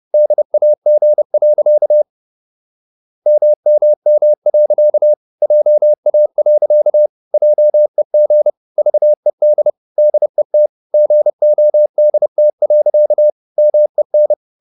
Morse code, heard as a keyed tone, about 30 wpm, 600 Hz. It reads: DAG. MMM. JA. JEG VED DET GODT. MEN
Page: Ferie 30 wpm